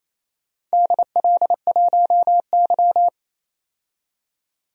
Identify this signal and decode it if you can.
Morse code, heard as a keyed tone, about 28 wpm, 700 Hz. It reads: DL1Y
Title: DXCC-100 Call Signs - 4 Characters - 28wpm